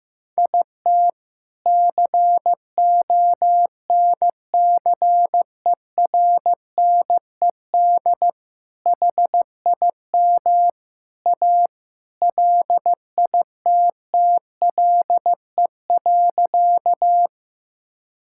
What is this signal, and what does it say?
Morse code, heard as a keyed tone, about 15 wpm, 700 Hz. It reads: IT CONCERNED HIM A LITTLE.
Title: Sentences from Top 1000 Words - Review Part 6 - 15wpm